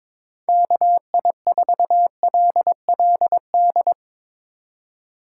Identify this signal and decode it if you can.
Morse code, heard as a keyed tone, about 22 wpm, 700 Hz. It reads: KI4LLD
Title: USA Call Signs - 4x Character Spacing - 22wpm